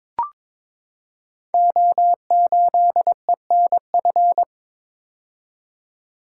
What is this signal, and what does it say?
Morse code, heard as a keyed tone, about 22 wpm, 700 Hz. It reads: O8ENF
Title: Code Groups - 5 Letter:Numbers - 22wpm